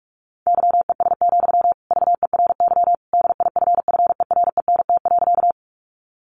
Morse code, heard as a keyed tone, about 46 wpm, 700 Hz. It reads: YES, VERY DIFFERENT.
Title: Sentences from Top 600 Words 45wpm